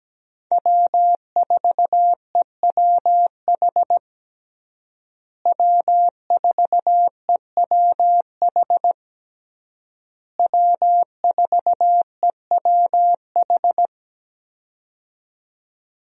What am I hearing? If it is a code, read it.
Morse code, heard as a keyed tone, about 17 wpm, 700 Hz. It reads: W4EWH W4EWH W4EWH
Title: USA Call Signs - 5 Characters - Repeated 3x - 17wpm